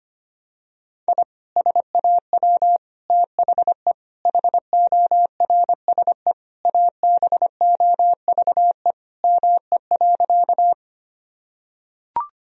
Morse code, heard as a keyed tone, about 25 wpm, 700 Hz. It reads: I SAW THE HORSE ABOVE ME.
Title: Sentences from Top 1000 Words - Review Part 2 - 25wpm